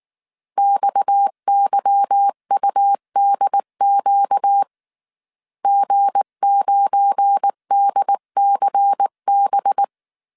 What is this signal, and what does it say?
Morse code, heard as a keyed tone, about 19 wpm, 800 Hz. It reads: XYUDQ G9DCB
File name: CW / DET / 1 9 1500C S B 18wpm 2551